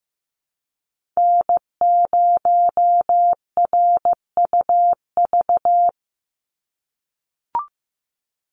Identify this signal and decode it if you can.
Morse code, heard as a keyed tone, about 15 wpm, 700 Hz. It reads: N0RUV
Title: USA Call Signs 15wpm